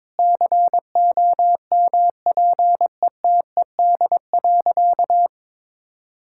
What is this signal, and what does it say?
Morse code, heard as a keyed tone, about 22 wpm, 700 Hz. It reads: COMPETED.
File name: practice / 22wpm / Sentences with 3-Word Phrases - 22wpm